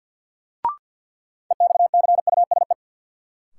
Morse code, heard as a keyed tone, about 52 wpm, 700 Hz. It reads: EXCUSE
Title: Sentences from Top 700 Words - New Word Drill - 50wpm